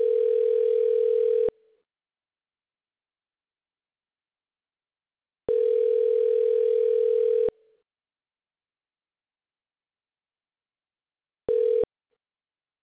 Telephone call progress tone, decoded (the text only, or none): ringback tone